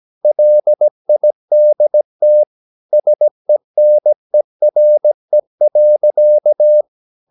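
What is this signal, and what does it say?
Morse code, heard as a keyed tone, about 17 wpm, 600 Hz. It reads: LIDT SENERE.
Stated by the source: Slow (QRS) Morse practice at 17 wpm, from CW med Gnister.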